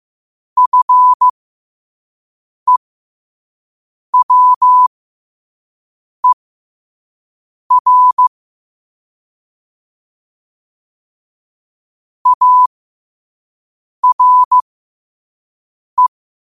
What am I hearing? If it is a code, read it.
Morse code, heard as a keyed tone, about 15 wpm, spaced as for 5 wpm, 1000 Hz. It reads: FEWER ARE
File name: qotdfiles / 5wpm